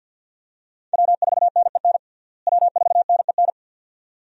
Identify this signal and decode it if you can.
Morse code, heard as a keyed tone, about 50 wpm, 700 Hz. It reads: W4NEN W4NEN
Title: USA Call Signs - 5 Characters - Repeated 2x - 50wpm